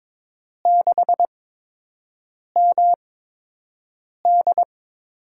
Morse code, heard as a keyed tone, about 23 wpm, spaced as for 6 wpm, 700 Hz. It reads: 6MD
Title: USA Call Signs - 8x Character Spacing - 22wpm